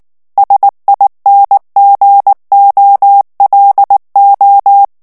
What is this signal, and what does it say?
Morse code, heard as a keyed tone, about 19 wpm, 800 Hz. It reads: SINGOLO
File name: morse mp3 / ITALIEN texte clair 2 vitesse 18 mots minute